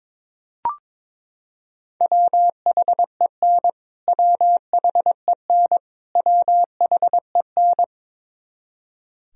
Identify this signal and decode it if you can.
Morse code, heard as a keyed tone, about 22 wpm, 700 Hz. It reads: WHEN WHEN WHEN
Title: Top 100 Words Repeated 22wpm